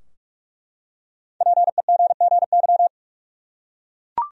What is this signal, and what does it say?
Morse code, heard as a keyed tone, about 47 wpm, 700 Hz. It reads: PEGGY
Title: USA Common Names - Max 5 Letters - 45wpm